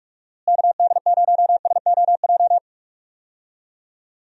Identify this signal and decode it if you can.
Morse code, heard as a keyed tone, about 41 wpm, 700 Hz. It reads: KD0SOJ
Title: USA Call Signs - 6 Characters - Repeated 2x - 45wpm